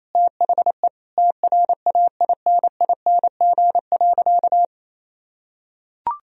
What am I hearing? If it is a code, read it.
Morse code, heard as a keyed tone, about 28 wpm, 700 Hz. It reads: THE TRAINING.
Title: Sentences from Top 1000 Words - Review Part 4 - 28wpm